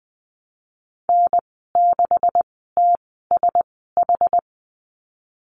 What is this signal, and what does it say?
Morse code, heard as a keyed tone, about 19 wpm, spaced as for 14 wpm, 700 Hz. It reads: N6TSH
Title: 213 Lesson 34g - Number 8 - Call Signs 20wpm